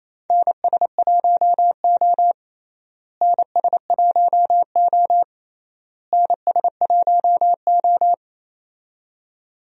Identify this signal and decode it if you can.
Morse code, heard as a keyed tone, about 28 wpm, 700 Hz. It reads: NS1O NS1O NS1O
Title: USA Call Signs - 4 Characters - Repeated 3x - 28wpm